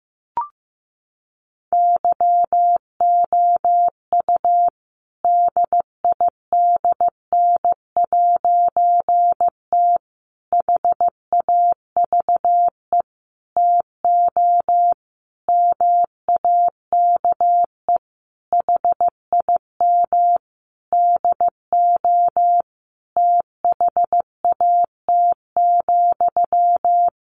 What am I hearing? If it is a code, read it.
Morse code, heard as a keyed tone, about 14 wpm, 700 Hz. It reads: YOU DIDN'T HAVE TO MAKE HIM DO THAT,
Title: A Sound Of Thunder 15wpm